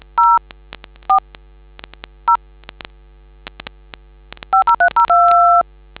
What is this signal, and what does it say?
DTMF, heard as telephone touch tones, tones of about 110 ms. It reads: 0 4 0 50302